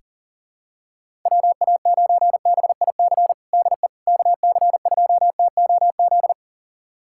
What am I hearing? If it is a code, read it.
Morse code, heard as a keyed tone, about 40 wpm, 700 Hz. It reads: WA9BIC DE KC2TOZ
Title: USA Callsign DE Callsign 40wpm